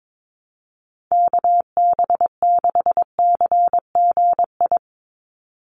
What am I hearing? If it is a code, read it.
Morse code, heard as a keyed tone, about 22 wpm, 700 Hz. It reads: KB6CGI
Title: USA Call Signs 22wpm